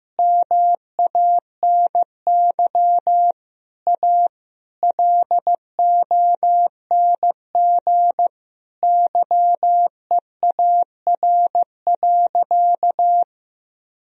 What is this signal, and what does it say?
Morse code, heard as a keyed tone, about 15 wpm, 700 Hz. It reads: MANY A LONG YEAR.
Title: Sentences from Top 1000 Words - Review Part 1 - 15wpm